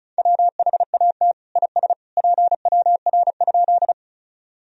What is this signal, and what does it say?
Morse code, heard as a keyed tone, about 36 wpm, 700 Hz. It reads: WHAT IS PWR?
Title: Sentences from Top 500 Words and Abbreviations 35wpm